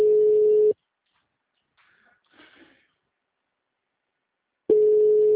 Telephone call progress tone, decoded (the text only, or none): ringback tone